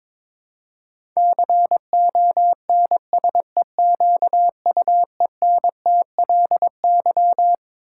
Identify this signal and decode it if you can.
Morse code, heard as a keyed tone, about 22 wpm, 700 Hz. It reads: CONSEQUENTLY
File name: Twelve Letter Words - 22wpm